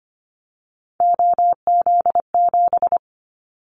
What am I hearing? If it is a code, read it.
Morse code, heard as a keyed tone, about 25 wpm, 700 Hz. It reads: OZ7
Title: Callsign Prefixes for Top 50 DXCC - 25wpm